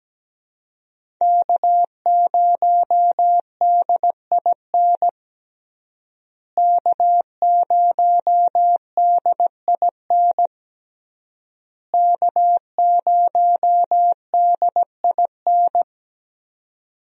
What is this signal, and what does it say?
Morse code, heard as a keyed tone, about 17 wpm, 700 Hz. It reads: K0DIN K0DIN K0DIN
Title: USA Call Signs - 5 Characters - Repeated 3x - 17wpm